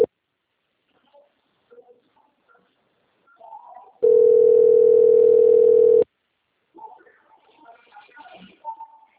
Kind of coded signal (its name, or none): telephone call progress tone